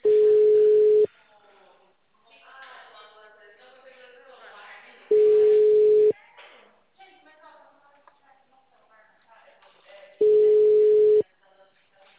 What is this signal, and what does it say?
Ringback tone (Europe, 425 Hz), with speech.
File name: out- -2005- - - .207